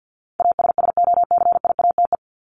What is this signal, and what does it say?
Morse code, heard as a keyed tone, about 50 wpm, 700 Hz. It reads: ASSOCIATE
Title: ED Suffix - Top 100 Words with Base Words 50wpm